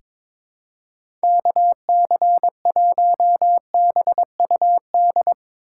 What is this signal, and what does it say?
Morse code, heard as a keyed tone, about 22 wpm, 700 Hz. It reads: KC1BUD
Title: USA Call Signs - 6 Characters - 6x Character Spacing - 22wpm